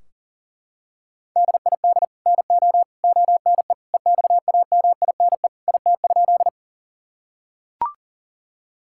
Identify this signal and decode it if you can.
Morse code, heard as a keyed tone, about 37 wpm, 700 Hz. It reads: DID NO ONE EXAMINE IT?
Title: Sentences from Top 1600 Words 40wpm